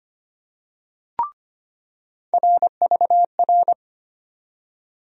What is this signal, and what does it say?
Morse code, heard as a keyed tone, about 25 wpm, 700 Hz. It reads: RVR
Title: USA Call Sign Postfixes - 3 Letters 25wpm